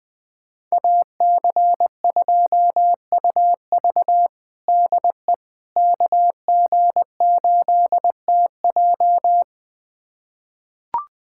Morse code, heard as a keyed tone, about 20 wpm, 700 Hz. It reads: AC2UV DE KG8TJ
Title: USA Callsign DE Callsign - 5 Characters - 20wpm